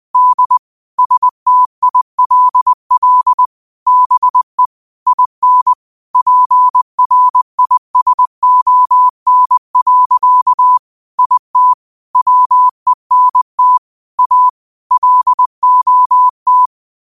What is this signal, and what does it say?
Morse code, heard as a keyed tone, about 20 wpm, 1000 Hz. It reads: D STILL BE IN PRISON. IT WENT A LOT
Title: 20 WPM morse code quotes for Thu, 29 Feb 2024